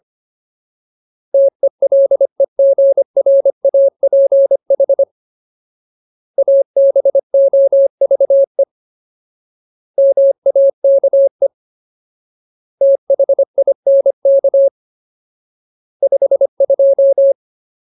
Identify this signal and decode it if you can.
Morse code, heard as a keyed tone, about 25 wpm, 550 Hz. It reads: TELEGRAPH ABOVE MAKE THINK 52
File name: Ultimate Word List / 25WPM Ultimate